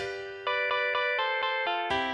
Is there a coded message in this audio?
No.